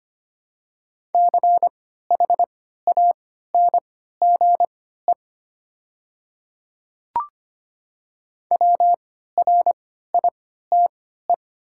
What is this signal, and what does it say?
Morse code, heard as a keyed tone, about 25 wpm, spaced as for 14 wpm, 690 Hz. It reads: CHANGE WRITE
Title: 3x Character Spacing - Top 200 Words - 25wpm